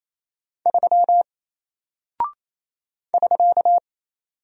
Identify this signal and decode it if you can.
Morse code, heard as a keyed tone, about 28 wpm, 700 Hz. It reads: 3 <SK>
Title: Single Character (Letter:Number:Punctuation:Prosign) Alt 28wpm